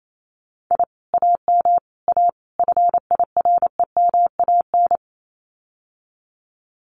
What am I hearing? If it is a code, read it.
Morse code, heard as a keyed tone, about 28 wpm, 700 Hz. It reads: I AM A FIREMAN
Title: Common QSO Phrases Encore 28wpm